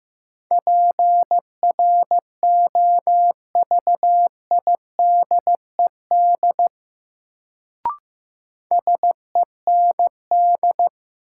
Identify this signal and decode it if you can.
Morse code, heard as a keyed tone, about 15 wpm, 700 Hz. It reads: PROVIDED SEND
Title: Sentences from Top 1500 Words - New Word Drill - 15wpm